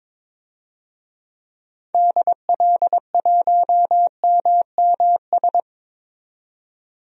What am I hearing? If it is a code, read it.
Morse code, heard as a keyed tone, about 22 wpm, 700 Hz. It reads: DL1MMS
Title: DXCC-100 Call Signs - 6 Characters - 22wpm